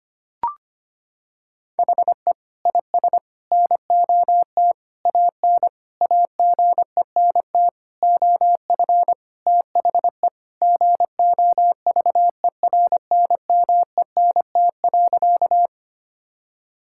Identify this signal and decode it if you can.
Morse code, heard as a keyed tone, about 25 wpm, 700 Hz. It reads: HE IS NOT AN AGENT OF THE GOVERNMENT.
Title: Sentences from Top 1600 Words 25wpm